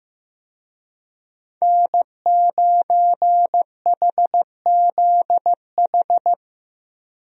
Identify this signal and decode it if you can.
Morse code, heard as a keyed tone, about 16 wpm, 700 Hz. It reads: N9HZH
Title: USA Call Signs - 5 Characters - Repeated 3x - 15wpm